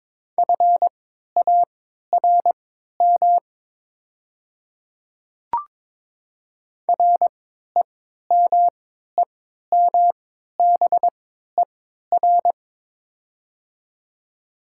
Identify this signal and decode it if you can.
Morse code, heard as a keyed tone, about 21 wpm, spaced as for 12 wpm, 700 Hz. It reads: FARM REMEMBER
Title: 3x Character Spacing - Top 500 Words - 22wpm